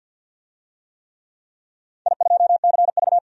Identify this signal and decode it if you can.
Morse code, heard as a keyed tone, about 47 wpm, 700 Hz. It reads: I2CV
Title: DXCC-100 Call Signs - 4 Characters - 50wpm